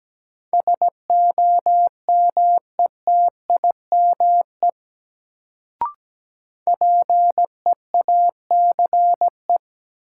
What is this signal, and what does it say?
Morse code, heard as a keyed tone, about 17 wpm, 700 Hz. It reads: SOMETIME PEACE
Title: Sentences from Top 1500 Words - New Word Drill Alt - 17wpm